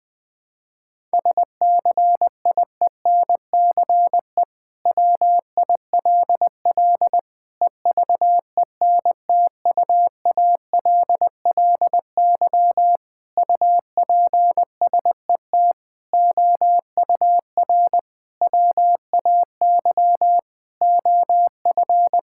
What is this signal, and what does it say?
Morse code, heard as a keyed tone, about 20 wpm, 700 Hz. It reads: SCIENCE WILL EVENTUALLY UPSET OUR WAY OF
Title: Sentences from Top 1000 Words - Review Part 6 - 20wpm